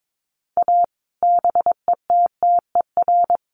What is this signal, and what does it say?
Morse code, heard as a keyed tone, about 22 wpm, 700 Hz. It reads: A BETTER
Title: Sets of 5 Words Encore2 22wpm